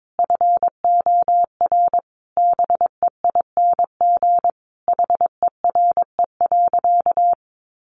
Morse code, heard as a keyed tone, about 22 wpm, 700 Hz. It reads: FOR BEING HERE.
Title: Sentences from Top 700 Words 22wpm